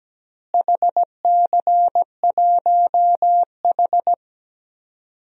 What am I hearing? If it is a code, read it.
Morse code, heard as a keyed tone, about 17 wpm, 700 Hz. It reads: HC1H
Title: DXCC-100 Call Signs - 4 Characters - Repeated 2x - 17wpm